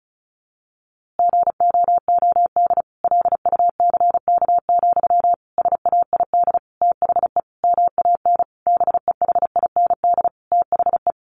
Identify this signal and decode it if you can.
Morse code, heard as a keyed tone, about 34 wpm, 700 Hz. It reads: GOOD LUCK, SAID THE MAN BEHIND THE
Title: A Sound Of Thunder 35wpm